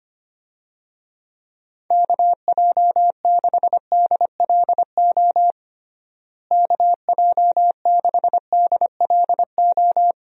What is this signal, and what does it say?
Morse code, heard as a keyed tone, about 25 wpm, 700 Hz. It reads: KJ6DLO KJ6DLO
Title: USA Call Signs - 6 Characters - Repeated 3x - 25wpm